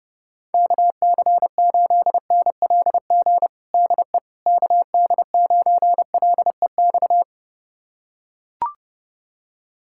Morse code, heard as a keyed tone, about 30 wpm, 700 Hz. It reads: KC8NLG DE KD9LEX
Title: USA Callsign DE Callsign - 6 Characters - 30wpm